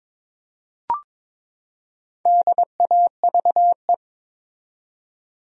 Morse code, heard as a keyed tone, about 22 wpm, 700 Hz. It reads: DAVE
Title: USA Common Names - Max 5 Letters - 22wpm